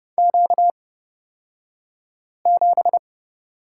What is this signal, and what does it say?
Morse code, heard as a keyed tone, about 30 wpm, spaced as for 5 wpm, 700 Hz. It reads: Q7
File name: Lesson 38d - Letter q - Two Characters 30wpm